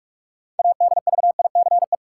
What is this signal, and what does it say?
Morse code, heard as a keyed tone, about 43 wpm, 700 Hz. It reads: ADVICE
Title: Sentences from Top 1100 Words - New Word Drill Alt - 45wpm